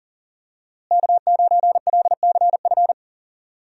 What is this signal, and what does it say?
Morse code, heard as a keyed tone, about 40 wpm, 700 Hz. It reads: K9LCF
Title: USA Call Signs - 5 Characters - Repeated 2x - 40wpm